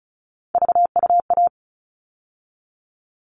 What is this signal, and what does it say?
Morse code, heard as a keyed tone, about 34 wpm, 700 Hz. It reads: VUA
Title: All Cut Numbers - 3-Digit 35wpm